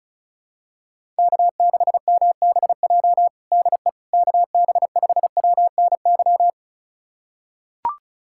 Morse code, heard as a keyed tone, about 36 wpm, 700 Hz. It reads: K6MBJ DE KB5WNY
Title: USA Callsign DE Callsign 35wpm